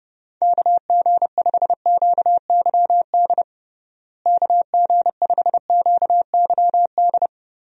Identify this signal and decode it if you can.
Morse code, heard as a keyed tone, about 30 wpm, 700 Hz. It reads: KG5QYD KG5QYD
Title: USA Call Signs - 6 Characters - Repeated 3x - 30wpm